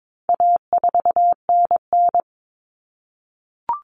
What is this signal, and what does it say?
Morse code, heard as a keyed tone, about 22 wpm, 700 Hz. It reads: A4NN
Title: Common Cut Numbers - 4-Digit 22wpm